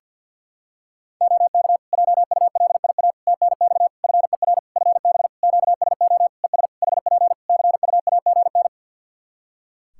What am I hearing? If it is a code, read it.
Morse code, heard as a keyed tone, about 53 wpm, 700 Hz. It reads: OK JULIA TNX FER FB QSO ES HP CUAGN
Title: Practice QSO - Exchange 4 - Ending - 50wpm